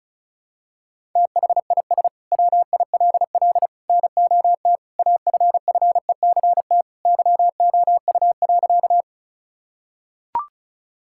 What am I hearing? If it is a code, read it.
Morse code, heard as a keyed tone, about 34 wpm, 700 Hz. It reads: THIS WILL NOT AFFECT YOU.
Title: Sentences from Top 1000 Words - Review Part 6 - 35wpm